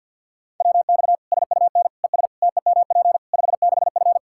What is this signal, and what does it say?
Morse code, heard as a keyed tone, about 53 wpm, 700 Hz. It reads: WX SUN ES TEMP 56F
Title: Practice QSO - Exchange 2 - Info - 50wpm